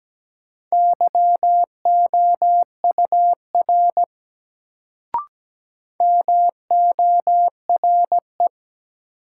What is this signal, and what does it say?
Morse code, heard as a keyed tone, about 17 wpm, 700 Hz. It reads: YOUR MORE
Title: Sentences from Top 100 Words - New Word Drill - 17wpm